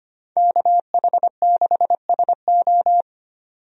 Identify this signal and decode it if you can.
Morse code, heard as a keyed tone, about 25 wpm, 700 Hz. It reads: KH6SO
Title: DXCC-100 Call Signs - 5 Characters - 25wpm